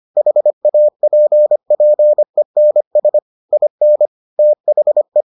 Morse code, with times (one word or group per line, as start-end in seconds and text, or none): 0.2-3.2 HAPPENS
3.5-4.1 IN
4.4-5.2 THE